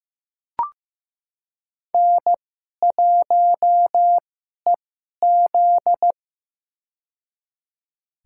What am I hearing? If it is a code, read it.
Morse code, heard as a keyed tone, about 15 wpm, spaced as for 11 wpm, 700 Hz. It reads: N1EZ
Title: USA Call Signs - 2x Character Spacing - 15wpm